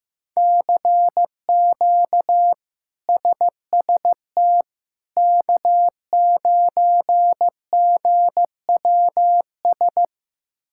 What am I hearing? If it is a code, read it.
Morse code, heard as a keyed tone, about 15 wpm, 700 Hz. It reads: CQ SST K9GWS
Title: K1USN SST QSO Practice - 15wpm